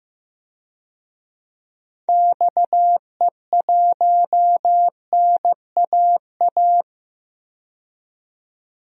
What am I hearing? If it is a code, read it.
Morse code, heard as a keyed tone, about 15 wpm, 700 Hz. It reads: XE1NAA
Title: DXCC-100 Call Signs - 6 Characters - 15wpm